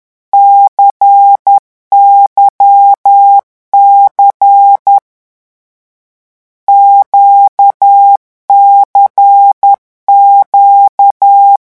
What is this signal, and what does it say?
Morse code, heard as a keyed tone, about 10 wpm, 800 Hz. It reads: CYC QCQ
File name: morse mp3 / lecon2-vitesse 10